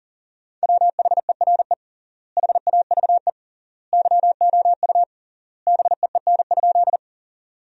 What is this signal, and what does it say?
Morse code, heard as a keyed tone, about 40 wpm, 700 Hz. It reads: WHERE HAVE YOU BEEN?
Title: 3x Word-Spacing - Sentences from Top 200 Words - 40wpm